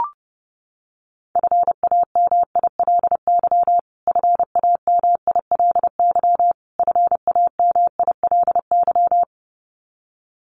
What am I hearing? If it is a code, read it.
Morse code, heard as a keyed tone, about 30 wpm, 700 Hz. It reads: FAMILY FAMILY FAMILY
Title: Top 500 Words Repeated 30wpm